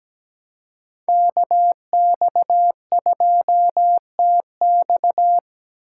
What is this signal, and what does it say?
Morse code, heard as a keyed tone, about 20 wpm, 700 Hz. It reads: KX2TX
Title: Speed Racing - USA 2x2 Call Signs 17wpm